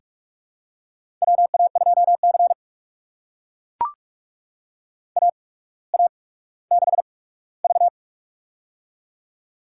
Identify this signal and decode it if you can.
Morse code, heard as a keyed tone, about 44 wpm, spaced as for 19 wpm, 700 Hz. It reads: WA2C AA6V
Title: USA Call Signs - 4 Characters - 8x Character Spacing - 45wpm